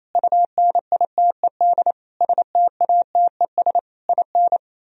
Morse code, heard as a keyed tone, about 37 wpm, 700 Hz. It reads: UNITED STATES IN